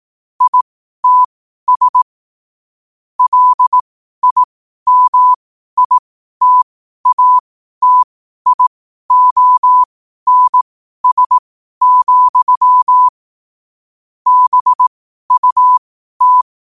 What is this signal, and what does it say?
Morse code, heard as a keyed tone, about 18 wpm, spaced as for 12 wpm, 1000 Hz. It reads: ITS LIMITATIONS, BUT
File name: qotdfiles / 12wpm